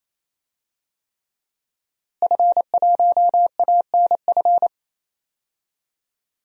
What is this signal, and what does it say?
Morse code, heard as a keyed tone, about 27 wpm, 700 Hz. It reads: F1ANF
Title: DXCC-100 Call Signs - 5 Characters - Repeated 3x - 28wpm